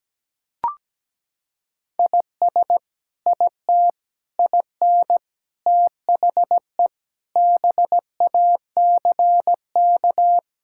Morse code, heard as a keyed tone, about 17 wpm, 700 Hz. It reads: IS IT IN THE BACK
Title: Sentences from Top 1000 Words - Review Part 5 - 17wpm